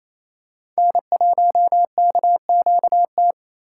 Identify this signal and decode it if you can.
Morse code, heard as a keyed tone, about 28 wpm, 700 Hz. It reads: N1KQT
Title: USA Call Signs - 5 Characters - 8x Character Spacing - 28wpm